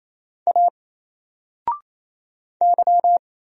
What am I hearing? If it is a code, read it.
Morse code, heard as a keyed tone, about 27 wpm, 700 Hz. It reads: A Y
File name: Single-Letter-Number-28wpm